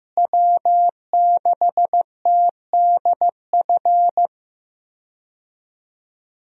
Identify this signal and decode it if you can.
Morse code, heard as a keyed tone, about 15 wpm, 700 Hz. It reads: W6TDF
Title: USA Call Signs - 5 Characters - Repeated 3x - 15wpm